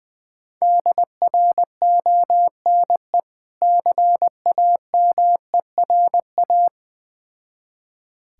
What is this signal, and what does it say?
Morse code, heard as a keyed tone, about 19 wpm, 700 Hz. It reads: DRONE CAMERA
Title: Difficult Words with Hints - 20wpm